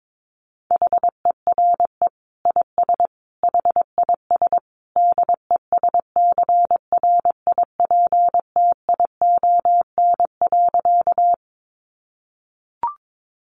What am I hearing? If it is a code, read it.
Morse code, heard as a keyed tone, about 22 wpm, 700 Hz. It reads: HERE IS HIS DESCRIPTION.
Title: Sentences from Top 1300 Words 22wpm